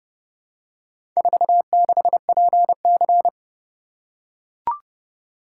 Morse code, heard as a keyed tone, about 29 wpm, 700 Hz. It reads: 46PC